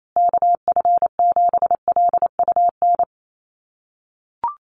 Morse code, heard as a keyed tone, about 28 wpm, 700 Hz. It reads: KF7LUN
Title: USA Call Signs Repeated 3x 28wpm